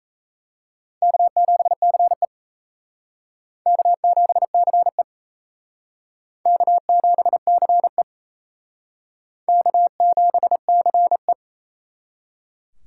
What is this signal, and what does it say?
Morse code, heard as a keyed tone, about 35 wpm, 700 Hz. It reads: K7CE K7CE K7CE K7CE
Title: Speed Racing - USA 1x2 Call Signs 28wpm